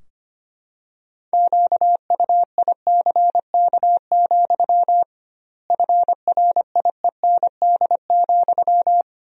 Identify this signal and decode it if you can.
Morse code, heard as a keyed tone, about 25 wpm, 700 Hz. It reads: QUICK, FRIEND,
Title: 2x Word-Spacing - Sentences from Top 500 Words - 25wpm